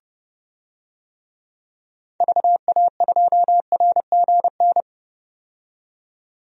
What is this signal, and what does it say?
Morse code, heard as a keyed tone, about 30 wpm, 700 Hz. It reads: VA2RGN
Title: DXCC-100 Call Signs - 6 Characters - 30wpm